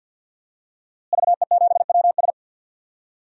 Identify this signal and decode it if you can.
Morse code, heard as a keyed tone, about 47 wpm, 700 Hz. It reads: VE7WS
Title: DXCC-100 Call Signs - 5 Characters - Repeated 2x - 50wpm